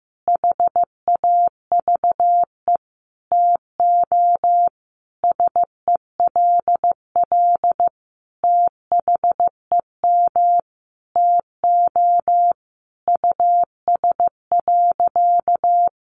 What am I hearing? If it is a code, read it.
Morse code, heard as a keyed tone, about 15 wpm, 700 Hz. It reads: HAVE TO SELL THEM TO US.
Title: Sentences from Top 1100 Words 15wpm